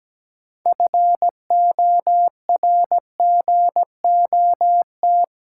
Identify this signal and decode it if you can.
Morse code, heard as a keyed tone, about 17 wpm, 700 Hz. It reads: FORGOT
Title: Sentences from Top 1400 Words - New Word Drill Alt - 17wpm